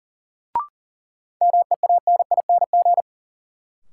Morse code, heard as a keyed tone, about 39 wpm, 700 Hz. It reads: MEANING
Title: Sentences from Top 900 Words - New Word Drill - 40wpm